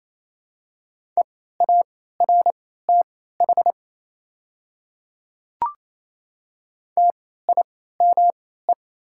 Morse code, heard as a keyed tone, about 27 wpm, spaced as for 16 wpm, 700 Hz. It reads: EARTH TIME
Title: 3x Character Spacing - Top 200 Words - 28wpm